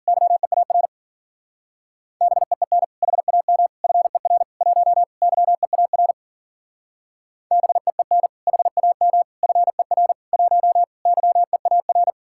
Morse code, heard as a keyed tone, about 47 wpm, 700 Hz. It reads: YEAR BEEN HAM FER 1 YEAR BEEN HAM FER 1 YEAR
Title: Speed-Racing Common QSO Phrases Encore - 40wpm